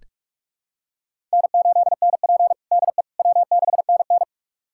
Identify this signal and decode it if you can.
Morse code, heard as a keyed tone, about 46 wpm, 700 Hz. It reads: N8NP DE W6NN
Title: USA Callsign DE Callsign - 4 Characters - 45wpm